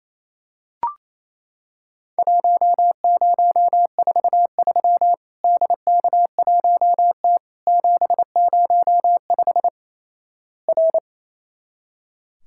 Morse code, heard as a keyed tone, about 28 wpm, 700 Hz. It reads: 1043 DK1T 705 R
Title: Worked All Europe DX Contest - QTC Exchange - 28wpm